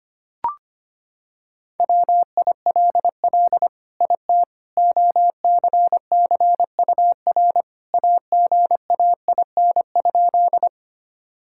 Morse code, heard as a keyed tone, about 25 wpm, 700 Hz. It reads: WILL IT OCCUR AGAIN?
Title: Sentences from Top 1000 Words - Review Part 6 - 25wpm